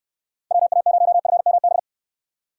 Morse code, heard as a keyed tone, about 69 wpm, 700 Hz. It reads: WA0FMB